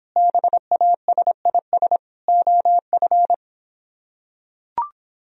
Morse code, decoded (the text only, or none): BASIS OF